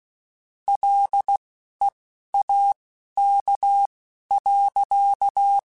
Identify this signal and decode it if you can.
Morse code, heard as a keyed tone, about 16 wpm, spaced as for 11 wpm, 800 Hz. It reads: LEAK.